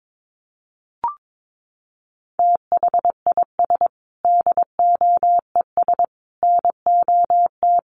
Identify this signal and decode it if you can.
Morse code, heard as a keyed tone, about 22 wpm, 700 Hz. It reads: THIS DOES NOT
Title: Sentences from Top 1400 Words 22wpm